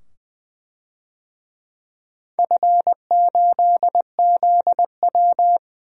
Morse code, heard as a keyed tone, about 20 wpm, 700 Hz. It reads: F8ZW
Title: DXCC-100 Call Signs - 4 Characters - 20wpm